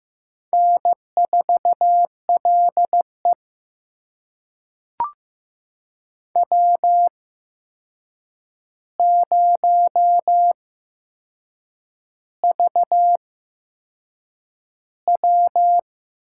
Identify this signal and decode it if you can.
Morse code, heard as a keyed tone, about 15 wpm, spaced as for 6 wpm, 700 Hz. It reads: N4LE W0VW
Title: USA Call Signs - 4 Characters - 8x Character Spacing - 15wpm